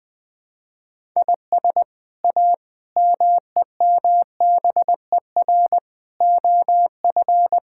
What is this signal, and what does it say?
Morse code, heard as a keyed tone, about 20 wpm, 700 Hz. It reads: IS A MEMBER OF
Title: Build A Phrase - 5-Word Phrases - 20wpm